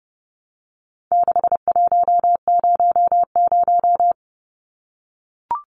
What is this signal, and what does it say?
Morse code, heard as a keyed tone, about 30 wpm, 700 Hz. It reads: 6100
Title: Numbers - 4-Digit 30wpm